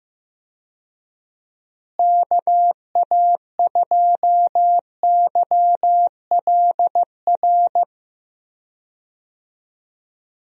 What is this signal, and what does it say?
Morse code, heard as a keyed tone, about 15 wpm, 700 Hz. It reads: KA2YLR
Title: USA Call Signs - 6 Characters - Repeated 3x - 15wpm